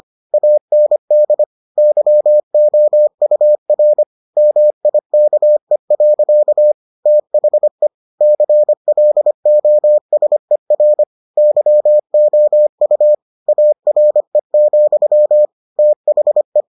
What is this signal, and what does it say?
Morse code, heard as a keyed tone, about 25 wpm, 600 Hz. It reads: AND YOUR MIKE. THE CLOSER YOU ARE, THE